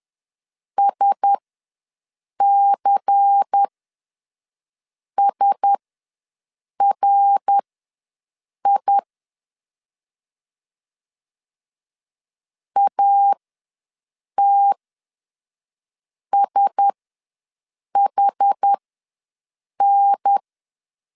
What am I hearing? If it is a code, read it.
Morse code, heard as a keyed tone, about 10 wpm, spaced as for 5 wpm, 800 Hz. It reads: SCSRI ATSHN
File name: LEZIONI / 5 LBVC / 5 LBVC ws 10wpm 3733